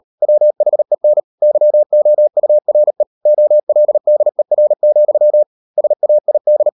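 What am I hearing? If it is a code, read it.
Morse code, heard as a keyed tone, about 38 wpm, 600 Hz. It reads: WHEN YOURE OLDER, SAID
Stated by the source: Morse code practice at 38 WPM, sending Ken Follett's novel Never in English, from CW med Gnister.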